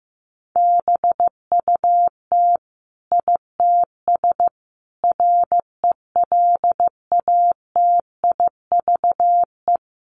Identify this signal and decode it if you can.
Morse code, heard as a keyed tone, about 15 wpm, 700 Hz. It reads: BUT ITS RELATIVE
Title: Sentences from Top 1400 Words 15wpm